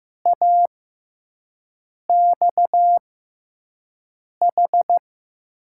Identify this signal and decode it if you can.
Morse code, heard as a keyed tone, about 16 wpm, spaced as for 5 wpm, 700 Hz. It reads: AXH
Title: USA Call Signs - 5 Characters - 6x Character Spacing - 15wpm